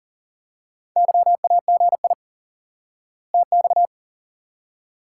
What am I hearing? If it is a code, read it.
Morse code, heard as a keyed tone, about 40 wpm, 700 Hz. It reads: YAGI TX
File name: IQE Course 3 - Lesson 89 - YAGI - Two Elements - 40wpm